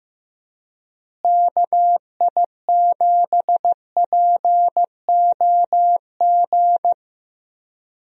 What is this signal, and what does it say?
Morse code, heard as a keyed tone, about 16 wpm, 700 Hz. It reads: KI7POG
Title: USA Call Signs - 6 Characters - Repeated 3x - 15wpm